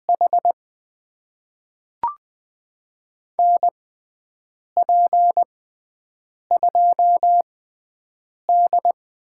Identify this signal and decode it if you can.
Morse code, heard as a keyed tone, about 20 wpm, spaced as for 7 wpm, 700 Hz. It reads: H NP2D
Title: USA Call Signs - 5 Characters - 6x Character Spacing - 20wpm